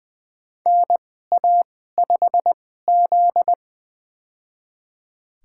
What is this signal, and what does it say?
Morse code, heard as a keyed tone, about 20 wpm, spaced as for 14 wpm, 700 Hz. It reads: NA5Z
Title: USA Call Signs - 4 Characters - 2x Character Spacing - 20wpm